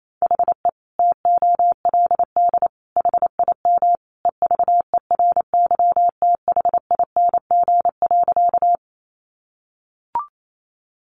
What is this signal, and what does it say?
Morse code, heard as a keyed tone, about 28 wpm, 700 Hz. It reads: HE TOLD HIM EVERYTHING.
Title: Sentences from Top 700 Words 28wpm